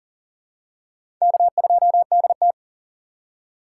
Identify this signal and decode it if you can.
Morse code, heard as a keyed tone, about 40 wpm, 700 Hz. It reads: K2DT
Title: USA Call Signs - 3x Character Spacing - 40wpm